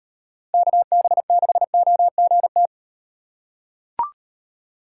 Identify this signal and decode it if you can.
Morse code, heard as a keyed tone, about 38 wpm, 700 Hz. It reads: KB6OGT